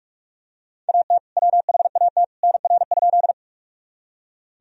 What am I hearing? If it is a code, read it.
Morse code, heard as a keyed tone, about 46 wpm, 700 Hz. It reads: AT WHAT NR?
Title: Sentences from Top 100 Words and Abbreviations 45wpm